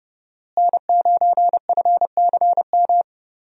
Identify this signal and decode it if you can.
Morse code, heard as a keyed tone, about 30 wpm, 700 Hz. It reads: N9FCM
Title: USA Call Signs - 5 Characters - Repeated 2x - 30wpm